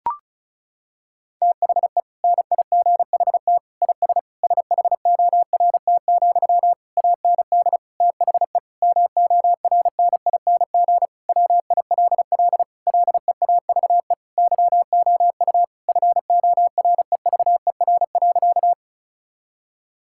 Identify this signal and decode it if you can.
Morse code, heard as a keyed tone, about 36 wpm, 700 Hz. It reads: THE NIGHT IS SHORT, AND THE MORNING WILL LEAVE YOU FOREVER.
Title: Sentences from Top 1600 Words 35wpm